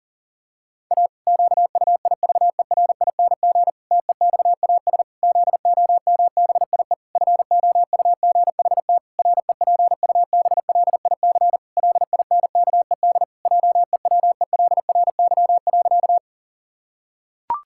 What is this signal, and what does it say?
Morse code, heard as a keyed tone, about 40 wpm, 700 Hz. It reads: A QUIVERING TEXAS ZOMBIE FOUGHT REPUBLIC LINKED JEWELRY.
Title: Pangram Challenge v2 - 40wpm